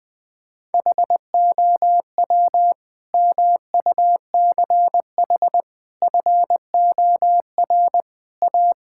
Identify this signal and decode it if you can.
Morse code, heard as a keyed tone, about 20 wpm, 700 Hz. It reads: HOW MUCH FOR A
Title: Sentences from Top 1000 Words - Review Part 5 - 20wpm